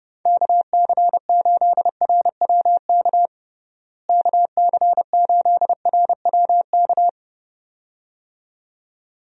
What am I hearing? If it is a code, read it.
Morse code, heard as a keyed tone, about 32 wpm, 700 Hz. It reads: KC8RWK KC8RWK
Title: USA Call Signs - 6 Characters - Repeated 2x - 30wpm